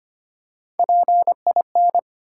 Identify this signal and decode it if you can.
Morse code, heard as a keyed tone, about 25 wpm, 700 Hz. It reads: PIN
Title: Three Letter Words - 25wpm